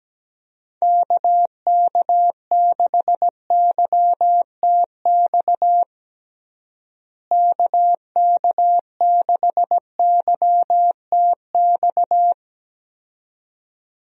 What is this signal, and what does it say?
Morse code, heard as a keyed tone, about 17 wpm, 700 Hz. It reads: KK6YTX KK6YTX
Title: USA Call Signs - 6 Characters - Repeated 2x - 17wpm